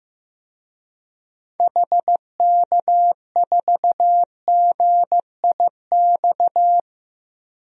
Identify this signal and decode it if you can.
Morse code, heard as a keyed tone, about 15 wpm, 700 Hz. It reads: HK4GIX
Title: DXCC-100 Call Signs - 6 Characters - 15wpm